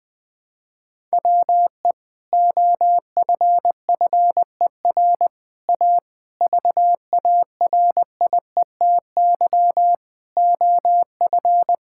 Morse code, heard as a keyed tone, about 20 wpm, 700 Hz. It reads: WE OFFER A VARIETY OF
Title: Sentences with 3-Word Phrases - Spoken Beginning - 20wpm